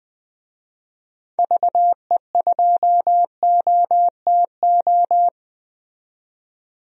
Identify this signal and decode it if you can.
Morse code, heard as a keyed tone, about 20 wpm, 700 Hz. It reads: VE2OTO
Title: DXCC-100 Call Signs - 6 Characters - 20wpm